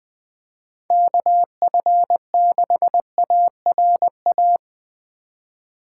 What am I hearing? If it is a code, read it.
Morse code, heard as a keyed tone, about 20 wpm, 700 Hz. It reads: KF6ARA
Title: USA Call Signs - 6 Characters - 4x Character Spacing - 20wpm